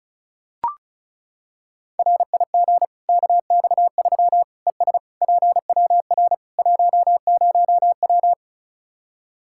Morse code, heard as a keyed tone, about 36 wpm, 700 Hz. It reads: RIG KX3 ES PWR 10W
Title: Practice QSO - Exchange 2 - Info - 35wpm